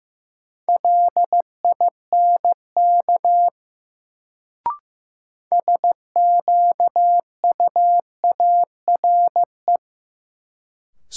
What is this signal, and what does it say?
Morse code, heard as a keyed tone, about 15 wpm, 700 Hz. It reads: LINK SQUARE
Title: Sentences from Top 1000 Words - Review Part 7 - New Word Drill - 15wpm